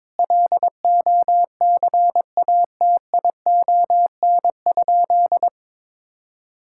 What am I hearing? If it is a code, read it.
Morse code, heard as a keyed tone, about 22 wpm, 700 Hz. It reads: LOCATION?
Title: Sentences from Top 1000 Words - Review Part 6 - 22wpm